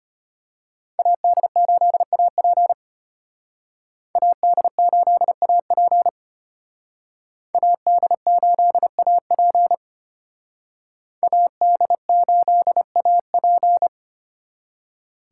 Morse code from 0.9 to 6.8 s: AD8AP AD8AP

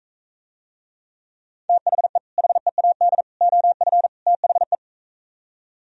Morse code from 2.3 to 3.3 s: HEAD

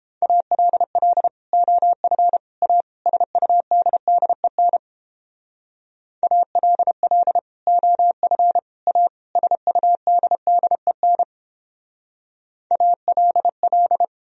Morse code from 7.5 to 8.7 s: OF

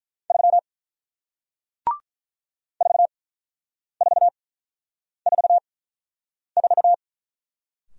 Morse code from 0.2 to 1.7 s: 3

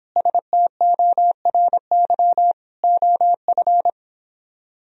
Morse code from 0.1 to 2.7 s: STORY

